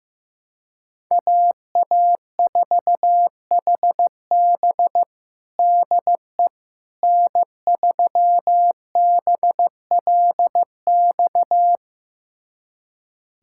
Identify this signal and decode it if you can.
Morse code, heard as a keyed tone, about 15 wpm, 700 Hz. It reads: AA4HB DE N3BLX
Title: USA Callsign DE Callsign - 5 Characters - 15wpm